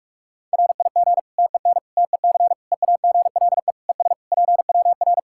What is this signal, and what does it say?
Morse code, heard as a keyed tone, about 46 wpm, 700 Hz. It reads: RIG TEN TEC EAGLE ES PWR
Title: Practice QSO - Exchange 2 - Info - 45wpm